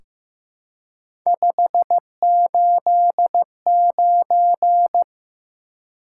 Morse code, heard as a keyed tone, about 15 wpm, 700 Hz. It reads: 589
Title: Top 180 QSO Elements 15wpm